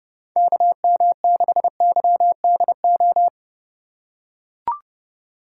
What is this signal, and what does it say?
Morse code, heard as a keyed tone, about 30 wpm, 700 Hz. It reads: KM6YDO
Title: USA Call Signs - 4x Character Spacing - 30wpm